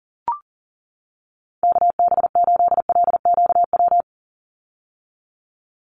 Morse code, heard as a keyed tone, about 40 wpm, 700 Hz. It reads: KB8LQW
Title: USA Call Signs 40wpm